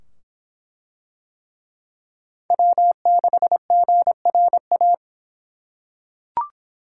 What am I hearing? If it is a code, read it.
Morse code, heard as a keyed tone, about 23 wpm, 700 Hz. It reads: W6GRA